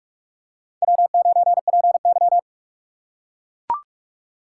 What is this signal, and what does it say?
Morse code, heard as a keyed tone, about 43 wpm, 700 Hz. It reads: W9PY